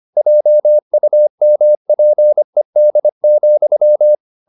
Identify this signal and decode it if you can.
Morse code, heard as a keyed tone, about 25 wpm, 600 Hz. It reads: JUMPED,
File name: the old man and the sea part 1 25wpm